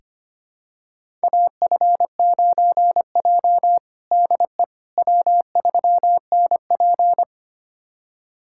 Morse code, heard as a keyed tone, about 25 wpm, 700 Hz. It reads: AF9J DE W3NP
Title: USA Callsign DE Callsign - 4 Characters - 25wpm